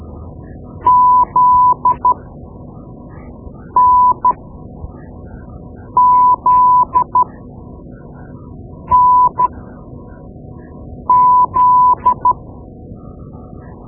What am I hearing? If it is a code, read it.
Morse code, heard as a keyed tone, about 13 wpm, spaced as for 4 wpm, 1000 Hz. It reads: ZNZNZ
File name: Morse Lessons / Morse Lessons / Tape-2B